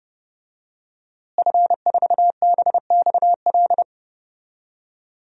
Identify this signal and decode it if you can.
Morse code, heard as a keyed tone, about 32 wpm, 700 Hz. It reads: F4BXL
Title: DXCC-100 Call Signs - 5 Characters - Repeated 3x - 30wpm